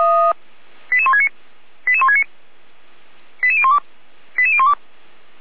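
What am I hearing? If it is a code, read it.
Five-tone selective call (ZVEI), heard as quick runs of five tones, tones of about 70 ms. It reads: 88169 88169 88012 88012
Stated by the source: ZVEI five-tone selective call 88267, received via SDR.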